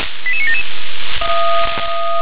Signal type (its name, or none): five-tone selective call